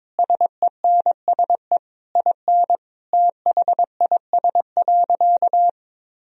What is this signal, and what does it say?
Morse code, heard as a keyed tone, about 22 wpm, 700 Hz. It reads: SENSE IN THIS.
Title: Sentences from Top 1000 Words - Review Part 4 - 22wpm